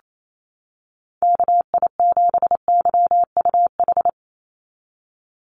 Morse code, heard as a keyed tone, about 28 wpm, 700 Hz. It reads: KI7YUH
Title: USA Call Signs 28wpm